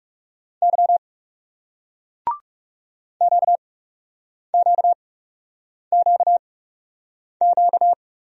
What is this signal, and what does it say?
Morse code, heard as a keyed tone, about 42 wpm, spaced as for 8 wpm, 700 Hz. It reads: Y QQQQ